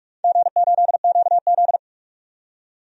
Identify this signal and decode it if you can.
Morse code, heard as a keyed tone, about 44 wpm, 700 Hz. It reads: G8QZ
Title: DXCC-100 Call Signs - 4 Characters - 45wpm